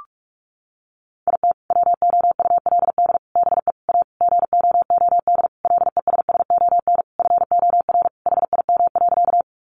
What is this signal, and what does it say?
Morse code, heard as a keyed tone, about 44 wpm, 700 Hz. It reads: IT WOULD BE A GOOD LESSON FOR HIM.
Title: Sentences from Top 1200 Words 45wpm